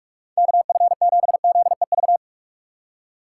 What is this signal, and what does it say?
Morse code, heard as a keyed tone, about 42 wpm, 700 Hz. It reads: KF7ZEV